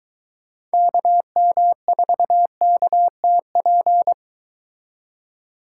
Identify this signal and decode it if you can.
Morse code, heard as a keyed tone, about 23 wpm, 700 Hz. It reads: KM4KTP